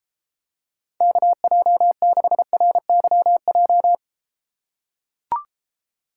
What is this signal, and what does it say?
Morse code, heard as a keyed tone, about 34 wpm, 700 Hz. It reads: KJ6RYJ